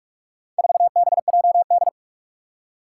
Morse code, heard as a keyed tone, about 44 wpm, 700 Hz. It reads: 4BJD
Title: Four Challenging Letter-Number Combinations 45wpm